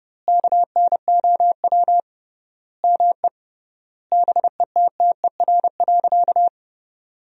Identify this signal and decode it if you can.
Morse code, heard as a keyed tone, about 29 wpm, 700 Hz. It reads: KNOW ME BETTER.
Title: 3x Word-Spacing - Sentences from Top 400 Words - 30wpm